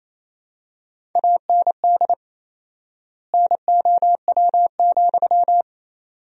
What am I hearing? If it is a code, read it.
Morse code, heard as a keyed tone, about 27 wpm, 700 Hz. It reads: AND NOW,
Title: 4x Word-Spacing - Sentences from Top 300 Words - 28wpm